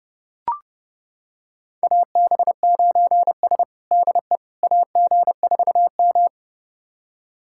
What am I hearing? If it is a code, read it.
Morse code, heard as a keyed tone, about 30 wpm, 700 Hz. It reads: AB9S DE AG4M
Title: USA Callsign DE Callsign - 4 Characters - 30wpm